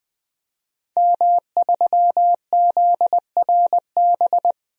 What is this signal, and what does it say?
Morse code, heard as a keyed tone, about 20 wpm, 700 Hz. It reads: M3ZRB
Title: DXCC-100 Call Signs - 5 Characters - Repeated 3x - 20wpm